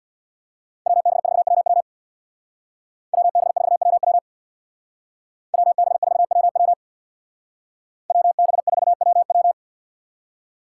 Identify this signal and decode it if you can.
Morse code, heard as a keyed tone, about 64 wpm, 700 Hz. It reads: WB4WW WB4WW WB4WW WB4WW